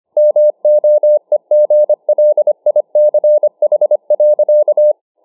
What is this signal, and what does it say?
Morse code, heard as a keyed tone, about 25 wpm, 600 Hz. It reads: MOEGLICH.